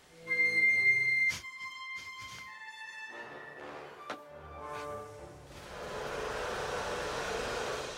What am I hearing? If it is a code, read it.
Dial-up modem handshake.